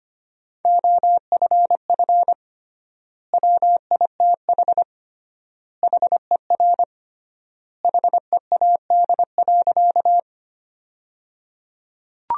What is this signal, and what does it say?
Morse code, heard as a keyed tone, about 25 wpm, 700 Hz. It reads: OFF WITH HER HEAD.
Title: 3x Word-Spacing - Sentences from Top 300 Words - 25wpm